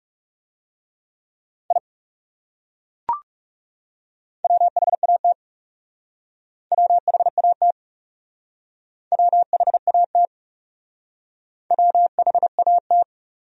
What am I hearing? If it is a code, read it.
Morse code, heard as a keyed tone, about 36 wpm, 700 Hz. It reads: I WHAT WHAT WHAT WHAT
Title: Speed-Racing Top 100 Words 30wpm